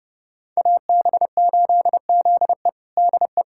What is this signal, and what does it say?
Morse code, heard as a keyed tone, about 32 wpm, 700 Hz. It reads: AB8ZE DE
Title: USA Callsign DE Callsign - 5 Characters - 30wpm